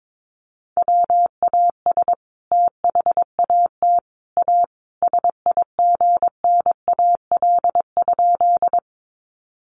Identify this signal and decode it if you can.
Morse code, heard as a keyed tone, about 22 wpm, 700 Hz. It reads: WAS THAT A SIGNAL?
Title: Sentences from Top 1400 Words 22wpm